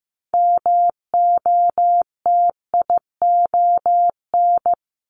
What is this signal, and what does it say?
Morse code, heard as a keyed tone, about 15 wpm, 700 Hz. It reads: MOTION
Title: Multi-letter Phonemes 15wpm